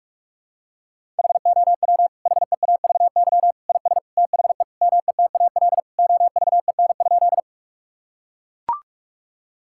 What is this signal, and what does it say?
Morse code, heard as a keyed tone, about 41 wpm, 700 Hz. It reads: HOW HEAVY IS THE METAL OVEN?
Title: Sentences from Top 1000 Words - Review Part 6 - 45wpm